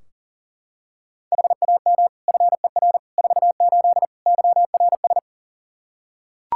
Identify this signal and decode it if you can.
Morse code, heard as a keyed tone, about 40 wpm, 700 Hz. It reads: HAM FER 48 YRS
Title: K1USN SST QSO Practice 40wpm